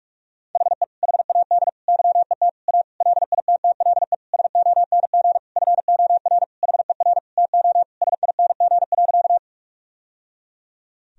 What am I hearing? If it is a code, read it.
Morse code, heard as a keyed tone, about 46 wpm, 700 Hz. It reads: HE HAD YET A LITTLE SONG FOR HER TO SING.
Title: Sentences from Top 1000 Words - Review Part 2 - 45wpm